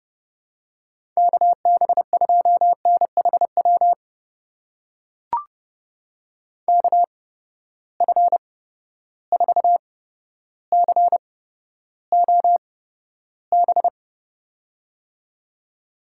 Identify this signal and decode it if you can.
Morse code, heard as a keyed tone, about 30 wpm, spaced as for 13 wpm, 700 Hz. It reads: KB2NHW KF4COB
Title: USA Call Signs - 6 Characters - 8x Character Spacing - 30wpm